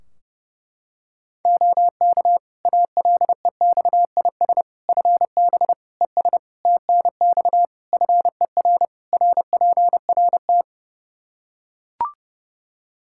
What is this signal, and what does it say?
Morse code, heard as a keyed tone, about 30 wpm, 700 Hz. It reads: OK ALEXIS FB ES TNX FER RPRT
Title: Practice QSO - Exchange 2 - Info - 30wpm